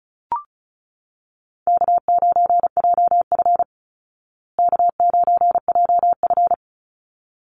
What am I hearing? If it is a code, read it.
Morse code, heard as a keyed tone, about 34 wpm, 700 Hz. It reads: K9JF K9JF
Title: Easy Call Signs Repeated 2x - 35wpm